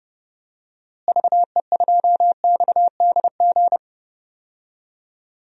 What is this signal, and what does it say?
Morse code, heard as a keyed tone, about 30 wpm, 700 Hz. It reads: VE2XDG
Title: DXCC-100 Call Signs - 6 Characters - Repeated 3x - 30wpm